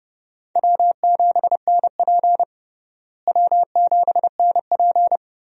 Morse code, heard as a keyed tone, about 32 wpm, 700 Hz. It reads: W7NP W7NP
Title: USA Call Signs - 4 Characters - Repeated 3x - 30wpm